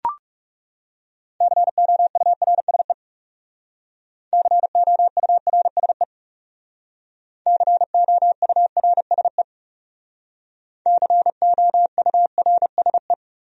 Morse code, heard as a keyed tone, about 38 wpm, 700 Hz. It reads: COURSE COURSE COURSE COURSE